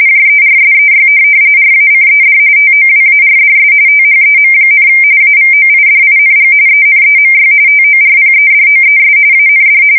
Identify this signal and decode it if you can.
RTTY, about 45.45 baud. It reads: JERSEY RACED TOWARDS THEM, AVOIDING ENEMY TORPEDOES AND
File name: fsk170 / USS-New-Jersey